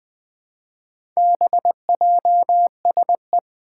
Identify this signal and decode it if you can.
Morse code, heard as a keyed tone, about 21 wpm, 700 Hz. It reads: BJSE
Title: Four Challenging Letter-Number Combinations 20wpm